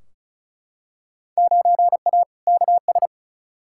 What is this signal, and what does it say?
Morse code, heard as a keyed tone, about 36 wpm, 700 Hz. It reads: 9A KS
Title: Field Day Exchanges Encore - 35wpm